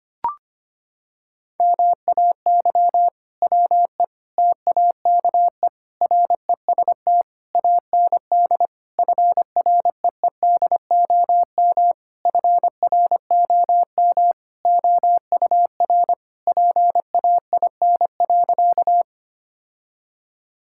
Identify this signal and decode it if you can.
Morse code, heard as a keyed tone, about 25 wpm, 700 Hz. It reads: MAY WE TAKE REST AND FREEDOM FROM OUR PAIN.
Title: Sentences from Top 1600 Words 25wpm